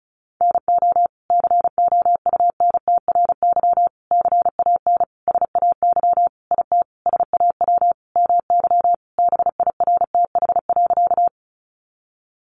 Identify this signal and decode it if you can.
Morse code, heard as a keyed tone, about 34 wpm, 700 Hz. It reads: NO COUNTRY CAN SAY IT SAW MY BIRTH.
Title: Sentences from Top 800 Words 35wpm